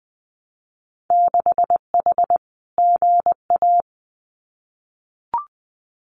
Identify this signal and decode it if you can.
Morse code, heard as a keyed tone, about 20 wpm, 700 Hz. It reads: 6H GA
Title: Winter Field Day Exchanges Encore 20wpm